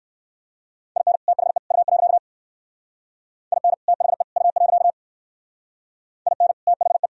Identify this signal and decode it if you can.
Morse code, heard as a keyed tone, about 56 wpm, 700 Hz. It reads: IN THE U. IN THE U. IN THE
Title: Speed-Racing - Sets of 3 Words 45wpm